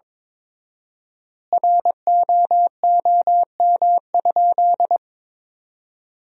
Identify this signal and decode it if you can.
Morse code, heard as a keyed tone, about 22 wpm, 700 Hz. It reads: ROOM?
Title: 4x Word-Spacing - Sentences from Top 300 Words - 22wpm